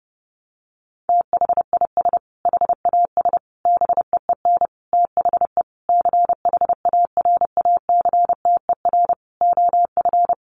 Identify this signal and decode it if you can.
Morse code, heard as a keyed tone, about 30 wpm, 700 Hz. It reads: THIS HAS BEEN THE CHARACTER OF
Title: Sentences from Top 1400 Words 30wpm